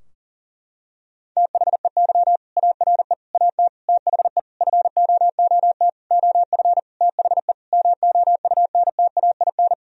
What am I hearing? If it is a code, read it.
Morse code, heard as a keyed tone, about 40 wpm, 700 Hz. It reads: THEY ARE AT THE FOOT OF THE MOUNTAIN
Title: Sentences from Top 1000 Words - Review Part 4 - 40wpm